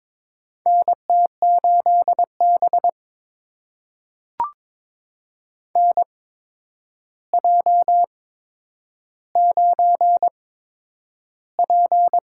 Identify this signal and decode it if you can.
Morse code, heard as a keyed tone, about 22 wpm, spaced as for 9 wpm, 700 Hz. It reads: NT8B NJ9P
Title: USA Call Signs - 4 Characters - 8x Character Spacing - 22wpm